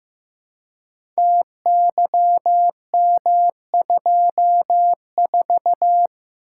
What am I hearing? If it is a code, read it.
Morse code, heard as a keyed tone, about 15 wpm, 700 Hz. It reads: TYM24
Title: Code Groups - 5 Letter:Numbers - 15wpm